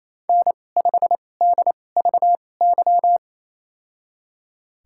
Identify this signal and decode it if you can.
Morse code, heard as a keyed tone, about 28 wpm, spaced as for 20 wpm, 700 Hz. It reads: N5DVY
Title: USA Call Signs - 5 Characters - 2x Character Spacing - 28wpm